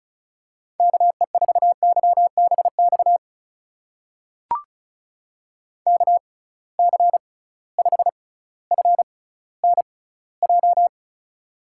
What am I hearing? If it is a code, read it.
Morse code, heard as a keyed tone, about 36 wpm, spaced as for 18 wpm, 700 Hz. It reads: KE4YBX KC5FNJ
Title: USA Call Signs - 6 Characters - 6x Character Spacing - 35wpm